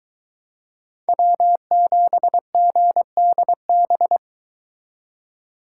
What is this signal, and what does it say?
Morse code, heard as a keyed tone, about 22 wpm, 700 Hz. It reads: W7GDB